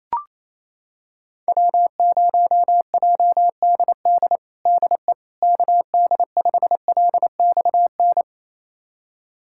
Morse code, heard as a keyed tone, about 28 wpm, 700 Hz. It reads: W0JDD DE KD5LXN
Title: USA Callsign DE Callsign 28wpm